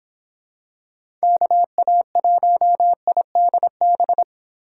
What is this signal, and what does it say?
Morse code, heard as a keyed tone, about 26 wpm, 700 Hz. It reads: KA1IDB